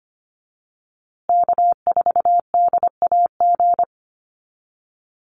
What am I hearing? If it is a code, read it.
Morse code, heard as a keyed tone, about 25 wpm, 700 Hz. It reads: K4DAG
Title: USA Call Signs Repeated 3x 25wpm